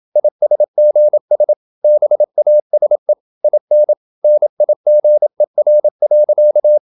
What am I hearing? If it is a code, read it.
Morse code, heard as a keyed tone, about 27 wpm, 600 Hz. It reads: ISGS BASE IN NIGER.
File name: Never 0027wpm